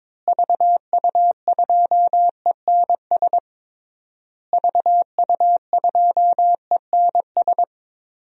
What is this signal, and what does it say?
Morse code, heard as a keyed tone, about 22 wpm, 700 Hz. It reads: VU2ENS VU2ENS
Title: DXCC-100 Call Signs - 6 Characters - Repeated 3x - 22wpm